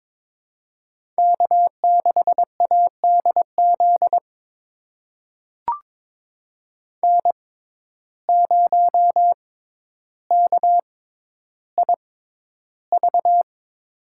Morse code, heard as a keyed tone, about 23 wpm, spaced as for 11 wpm, 700 Hz. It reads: K6ADZ N0KIV
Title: USA Call Signs - 5 Characters - 6x Character Spacing - 22wpm